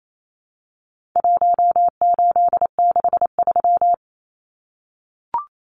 Morse code, heard as a keyed tone, about 28 wpm, 700 Hz. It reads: 1863
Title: Numbers - Contest Style - 4-Digit 28wpm